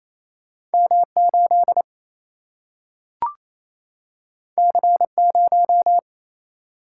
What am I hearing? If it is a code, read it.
Morse code, heard as a keyed tone, about 28 wpm, 700 Hz. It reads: M 8 C 0
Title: Single-Letter Single-Number 28wpm